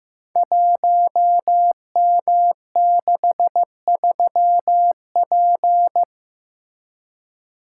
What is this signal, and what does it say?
Morse code, heard as a keyed tone, about 15 wpm, 700 Hz. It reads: 1M63P
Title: Code Groups - 5 Letter:Numbers - 15wpm